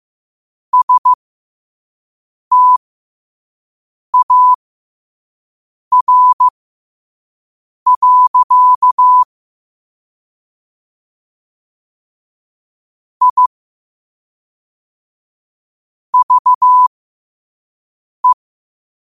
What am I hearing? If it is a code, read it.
Morse code, heard as a keyed tone, about 15 wpm, spaced as for 5 wpm, 1000 Hz. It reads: STAR. I VE
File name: qotdfiles / 5wpm